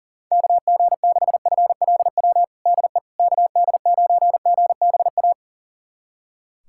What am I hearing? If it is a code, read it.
Morse code, heard as a keyed tone, about 40 wpm, 700 Hz. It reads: KG6FLW DE KD9GBA
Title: USA Callsign DE Callsign 40wpm